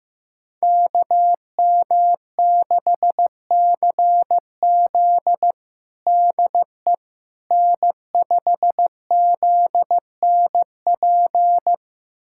Morse code, heard as a keyed tone, about 15 wpm, 700 Hz. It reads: KM6CZ DE N5ZNP
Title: USA Callsign DE Callsign - 5 Characters - 15wpm